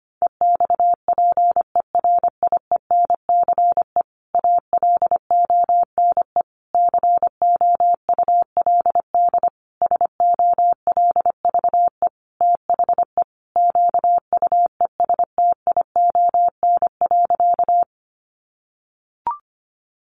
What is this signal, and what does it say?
Morse code, heard as a keyed tone, about 25 wpm, 700 Hz. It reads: EXPERIENCE ALONE COULD SOLVE THE QUESTION.
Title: Sentences from Top 1000 Words 25wpm